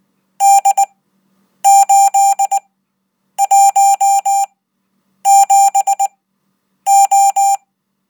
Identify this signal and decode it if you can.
Morse code, heard as a keyed tone, about 20 wpm, spaced as for 8 wpm, 790 Hz. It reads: D817O